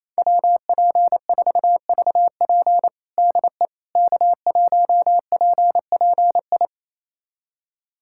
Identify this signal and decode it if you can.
Morse code, heard as a keyed tone, about 28 wpm, 700 Hz. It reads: WP4VP DE K1PPI
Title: USA Callsign DE Callsign - 5 Characters - 28wpm